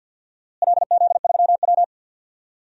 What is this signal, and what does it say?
Morse code, heard as a keyed tone, about 50 wpm, 700 Hz. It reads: LZ3W
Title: DXCC-100 Call Signs - 4 Characters - Repeated 3x - 50wpm